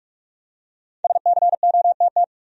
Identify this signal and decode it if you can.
Morse code, heard as a keyed tone, about 46 wpm, 700 Hz. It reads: SCOTT
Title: USA Common Names - 45wpm